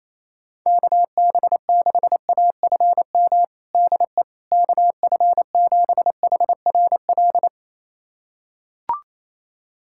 Morse code, heard as a keyed tone, about 28 wpm, 700 Hz. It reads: KB6AFM DE KF7HRL
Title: USA Callsign DE Callsign - 6 Characters - 28wpm